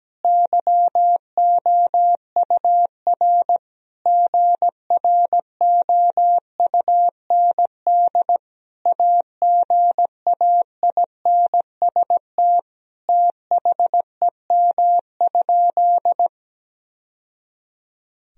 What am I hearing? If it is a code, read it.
Morse code, heard as a keyed tone, about 17 wpm, 700 Hz. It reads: YOUR GROUND AGAINST THEM?
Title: Sentences from Top 1000 Words - Review Part 2 - 17wpm